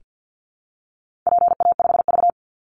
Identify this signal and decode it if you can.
Morse code, heard as a keyed tone, about 48 wpm, 700 Hz. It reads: PA5V
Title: Call Signs Repeated 3x - 50wpm